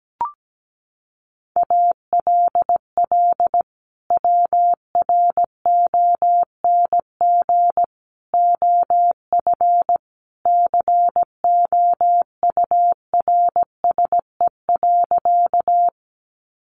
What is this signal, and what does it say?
Morse code, heard as a keyed tone, about 17 wpm, 700 Hz. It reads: ALL WRONG OF COURSE.
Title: Sentences from Top 900 Words 17wpm